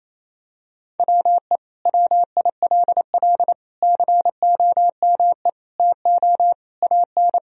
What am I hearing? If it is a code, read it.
Morse code, heard as a keyed tone, about 27 wpm, 690 Hz. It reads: WE WILL COME TO AN
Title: Sentences from Top 1000 Words - Review Part 3 - 28wpm